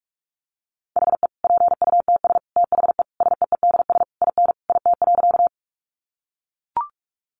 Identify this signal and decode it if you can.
Morse code, heard as a keyed tone, about 44 wpm, 700 Hz. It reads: HE PUTS THE SEEDS IN IT.
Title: Easy Phrases 45wpm